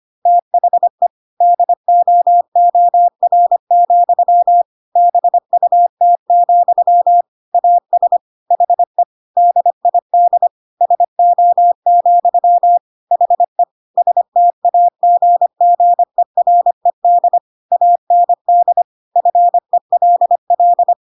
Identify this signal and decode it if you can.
Morse code, heard as a keyed tone, about 25 wpm, 700 Hz. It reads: THE DOOR, BUT, AS HE DID SO, HE STAGGERED AND FELL
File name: Pinocchio / Pinocchio ch7 25 25 ik1hge